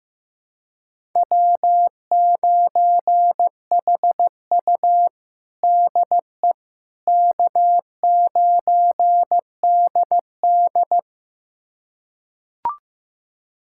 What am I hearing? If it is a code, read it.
Morse code, heard as a keyed tone, about 15 wpm, 700 Hz. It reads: W9HU DE K9DD
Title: USA Callsign DE Callsign - 4 Characters - 15wpm